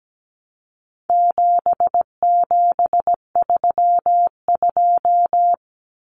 Morse code, heard as a keyed tone, about 17 wpm, 700 Hz. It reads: 7732
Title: Common Cut Numbers - 4-Digit 17wpm